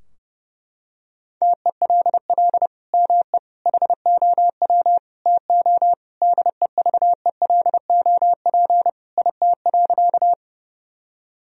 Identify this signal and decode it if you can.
Morse code, heard as a keyed tone, about 30 wpm, 700 Hz. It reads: TELL ME HOW TO DEVELOP IT.
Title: Sentences from Top 1000 Words - Review Part 2 - 30wpm